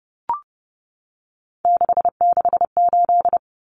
Morse code, heard as a keyed tone, about 30 wpm, 700 Hz. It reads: 668
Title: Numbers - 3-Digit 30wpm